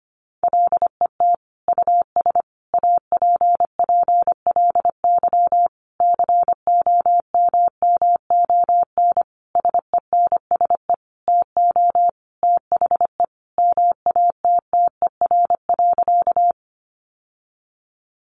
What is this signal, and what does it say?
Morse code, heard as a keyed tone, about 25 wpm, 700 Hz. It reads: LET US APPLY COMMON SENSE TO THE MATTER.
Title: Sentences from Top 1200 Words 25wpm